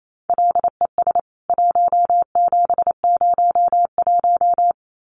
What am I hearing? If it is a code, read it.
Morse code, heard as a keyed tone, about 28 wpm, 700 Hz. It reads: LES 1701
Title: CWT Exchange 28wpm